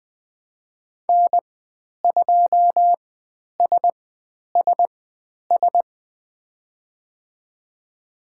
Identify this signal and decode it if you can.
Morse code, heard as a keyed tone, about 20 wpm, spaced as for 10 wpm, 700 Hz. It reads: N2SSS
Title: 102 Lesson 17g - Letter u - Call Signs 20wpm